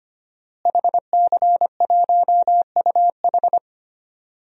Morse code, heard as a keyed tone, about 25 wpm, 700 Hz. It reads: HC1UH
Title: DXCC-100 Call Signs - 5 Characters - Repeated 3x - 25wpm